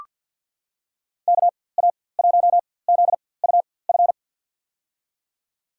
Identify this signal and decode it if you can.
Morse code, heard as a keyed tone, about 47 wpm, spaced as for 23 wpm, 700 Hz. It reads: KA1ZUF
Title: USA Call Signs - 6 Characters - 4x Character Spacing - 50wpm